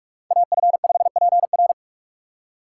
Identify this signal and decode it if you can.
Morse code, heard as a keyed tone, about 46 wpm, 700 Hz. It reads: AF5PR
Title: USA Call Signs - 5 Characters - 2x Character Spacing - 45wpm